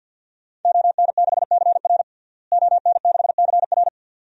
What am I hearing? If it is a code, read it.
Morse code, heard as a keyed tone, about 50 wpm, 700 Hz. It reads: ON6CR ON6CR
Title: DXCC-100 Call Signs - 5 Characters - Repeated 2x - 50wpm